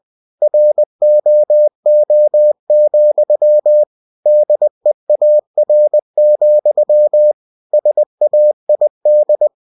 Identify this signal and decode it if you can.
Morse code, heard as a keyed tone, about 20 wpm, 600 Hz. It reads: ROO, DEAR, SAID
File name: cw-books / winnie-the-pooh / winnie-20wpm-page-0072